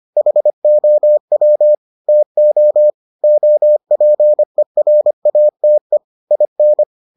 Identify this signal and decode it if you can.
Morse code, heard as a keyed tone, about 25 wpm, 600 Hz. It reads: HOW TO OPERATE IN